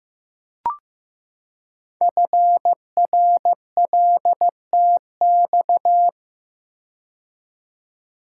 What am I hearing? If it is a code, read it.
Morse code, heard as a keyed tone, about 15 wpm, 700 Hz. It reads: FRLTX
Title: Code Groups - 5 Letters - 15wpm